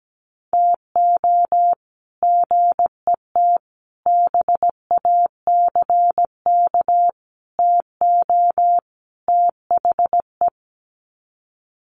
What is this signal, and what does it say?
Morse code, heard as a keyed tone, about 17 wpm, 700 Hz. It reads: TO GET BACK TO THE
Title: Sets of 5 Words Encore 17wpm